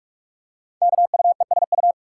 Morse code, heard as a keyed tone, about 46 wpm, 700 Hz. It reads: KUESU
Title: Code Groups - 5 Letter:Numbers - 45wpm